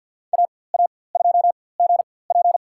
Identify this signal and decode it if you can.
Morse code, heard as a keyed tone, about 50 wpm, spaced as for 23 wpm, 700 Hz. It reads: AA2GP
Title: USA Call Signs - 5 Characters - 4x Character Spacing - 50wpm